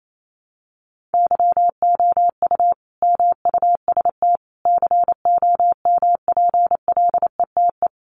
Morse code, heard as a keyed tone, about 28 wpm, 700 Hz. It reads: YOU MUST COMPLETE
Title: Sentences from Top 1400 Words 28wpm